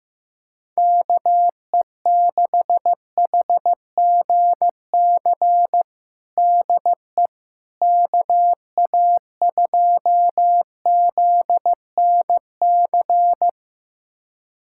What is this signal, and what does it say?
Morse code, heard as a keyed tone, about 15 wpm, 700 Hz. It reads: KE6HGC DE KA2ZNC
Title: USA Callsign DE Callsign - 6 Characters - 15wpm